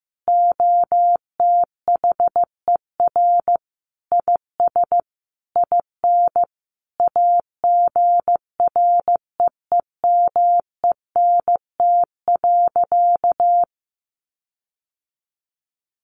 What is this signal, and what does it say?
Morse code, heard as a keyed tone, about 15 wpm, 700 Hz. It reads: OTHER IS IN AGREEMENT.
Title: Sentences from Top 1500 Words 15wpm